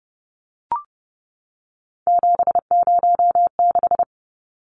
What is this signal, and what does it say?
Morse code, heard as a keyed tone, about 30 wpm, 700 Hz. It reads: 706
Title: Numbers - 3-Digit 30wpm